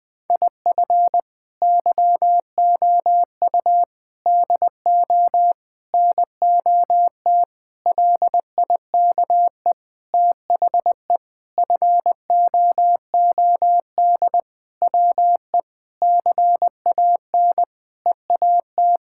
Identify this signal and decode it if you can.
Morse code, heard as a keyed tone, about 20 wpm, 700 Hz. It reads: IF YOU DO NOT LIKE THE FOOD WE CAN EAT
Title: Sentences with 5-Word Phrases - 20wpm